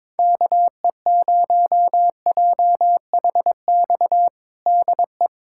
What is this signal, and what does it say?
Morse code, heard as a keyed tone, about 22 wpm, 700 Hz. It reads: KE0JHX DE
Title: USA Callsign DE Callsign - 6 Characters - 22wpm